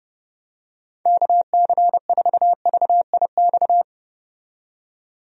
Morse code, heard as a keyed tone, about 30 wpm, 700 Hz. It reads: KC4VIX
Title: USA Call Signs - 6 Characters - 30wpm